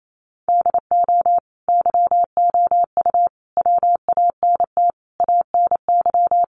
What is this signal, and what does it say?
Morse code, heard as a keyed tone, about 28 wpm, 700 Hz. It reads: DO YOU WANT ANY
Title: Sentences from Top 1400 Words 28wpm